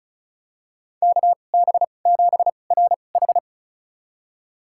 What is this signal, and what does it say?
Morse code, heard as a keyed tone, about 36 wpm, spaced as for 25 wpm, 700 Hz. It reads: KB7RH
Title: USA Call Signs - 2x Character Spacing - 35wpm